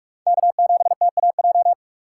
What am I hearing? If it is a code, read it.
Morse code, heard as a keyed tone, about 43 wpm, 690 Hz. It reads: K7TAJ
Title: USA Call Signs - 4x Character Spacing - 45wpm